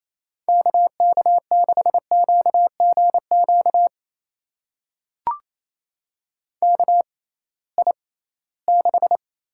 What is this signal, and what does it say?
Morse code, heard as a keyed tone, about 28 wpm, spaced as for 18 wpm, 700 Hz. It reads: KK6QGQ KI6
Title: USA Call Signs - 6 Characters - 6x Character Spacing - 28wpm